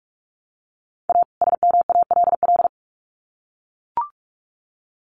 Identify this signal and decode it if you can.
Morse code, heard as a keyed tone, about 44 wpm, 700 Hz. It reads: A SMALL
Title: Sets of 2 Words 45wpm